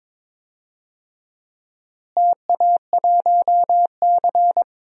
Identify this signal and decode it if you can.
Morse code, heard as a keyed tone, about 21 wpm, 700 Hz. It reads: TA1C
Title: DXCC-100 Call Signs - 4 Characters - Repeated 3x - 22wpm